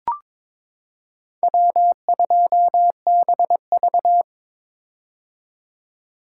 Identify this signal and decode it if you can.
Morse code, heard as a keyed tone, about 22 wpm, 700 Hz. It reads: W2BV
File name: practice / 22wpm / Four Letter-Number Combinations - 22wpm